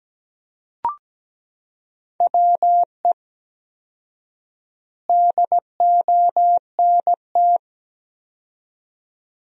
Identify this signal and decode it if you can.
Morse code, heard as a keyed tone, about 17 wpm, 700 Hz. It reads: WE DONT
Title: 4x Word-Spacing - Sentences from Top 400 Words - 17wpm